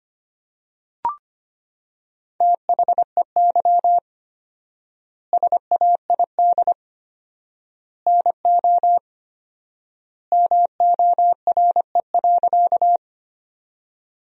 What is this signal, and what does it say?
Morse code, heard as a keyed tone, about 25 wpm, 700 Hz. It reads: THEY SAID NO MORE.
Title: 4x Word-Spacing - Sentences from Top 100 Words - 25wpm